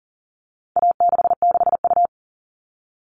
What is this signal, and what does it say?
Morse code, heard as a keyed tone, about 40 wpm, 700 Hz. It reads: A66U
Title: All Cut Numbers - 4-Digit 40wpm